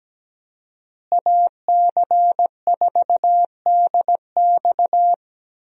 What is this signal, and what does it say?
Morse code, heard as a keyed tone, about 17 wpm, 700 Hz. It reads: AC4DX
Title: USA Call Signs - 5 Characters - Repeated 3x - 17wpm